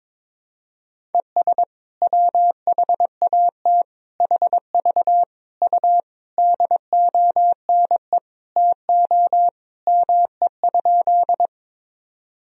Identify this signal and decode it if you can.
Morse code, heard as a keyed tone, about 22 wpm, 700 Hz. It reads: ES WHAT HV U DONE TO ME?
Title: Sentences from Top 500 Words and Abbreviations 22wpm